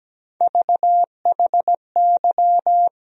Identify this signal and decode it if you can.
Morse code, heard as a keyed tone, about 17 wpm, 700 Hz. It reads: VHY
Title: USA Call Sign Postfixes - 3 Letters 17wpm